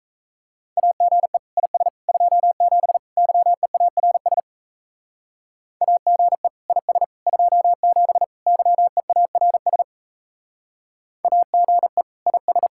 Morse code, heard as a keyed tone, about 32 wpm, 700 Hz. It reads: AGE IS 27 YEARS AGE IS 27 YEARS AGE IS
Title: Speed-Racing Common QSO Phrases Encore - 28wpm